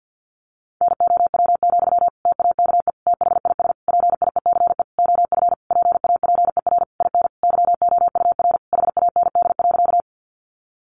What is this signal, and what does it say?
Morse code, heard as a keyed tone, about 50 wpm, 700 Hz. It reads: NOW, TAKE THIS PIECE OF PAPER IN YOUR HAND.
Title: Sentences from Top 400 Words 50wpm